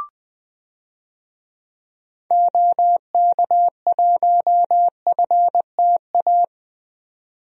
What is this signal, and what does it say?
Morse code, heard as a keyed tone, about 20 wpm, 700 Hz. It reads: OK1FTA
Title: DXCC-100 Call Signs - 6 Characters - 20wpm